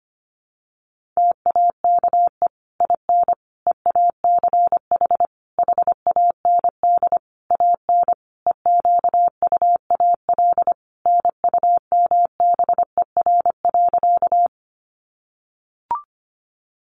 Morse code, heard as a keyed tone, about 25 wpm, 700 Hz. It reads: TAKE IN EACH HAND AN EQUAL NUMBER.
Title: Sentences from Top 1400 Words 25wpm